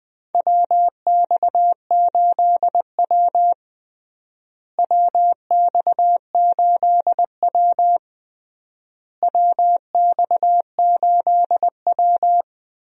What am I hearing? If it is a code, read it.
Morse code, heard as a keyed tone, about 20 wpm, 700 Hz. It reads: WX8W WX8W WX8W
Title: USA Call Signs - 4 Characters - Repeated 3x - 20wpm